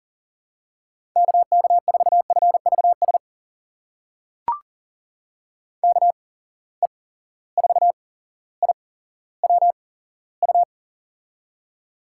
Morse code, heard as a keyed tone, about 39 wpm, spaced as for 17 wpm, 700 Hz. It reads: KK4FVS KE4IWU
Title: USA Call Signs - 6 Characters - 8x Character Spacing - 40wpm